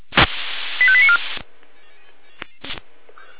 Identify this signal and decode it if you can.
Five-tone selective call (ZVEI), heard as quick runs of five tones, tones of about 70 ms. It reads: 85894